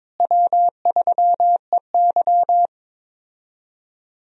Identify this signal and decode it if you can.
Morse code, heard as a keyed tone, about 22 wpm, 700 Hz. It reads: W3EY
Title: USA Call Signs - 4 Characters - Repeated 2x - 22wpm